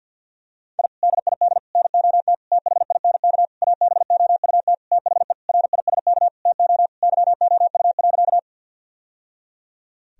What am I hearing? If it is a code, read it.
Morse code, heard as a keyed tone, about 47 wpm, 700 Hz. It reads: I DID NOT THINK ABOUT THE RISK TO YOU.
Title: Sentences from Top 1000 Words - Review Part 4 - 50wpm